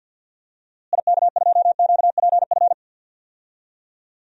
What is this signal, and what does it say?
Morse code, heard as a keyed tone, about 53 wpm, 700 Hz. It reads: IK2QPF
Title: DXCC-100 Call Signs - 6 Characters - Repeated 2x - 50wpm